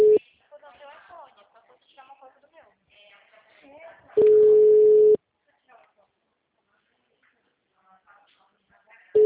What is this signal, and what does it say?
Ringback tone (Europe, 425 Hz), with speech.
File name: out- -2010- - - .27562